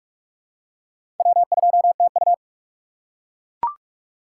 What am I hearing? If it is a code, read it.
Morse code, heard as a keyed tone, about 42 wpm, 700 Hz. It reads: W2TU
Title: USA Call Signs - 4 Characters - 6x Character Spacing - 45wpm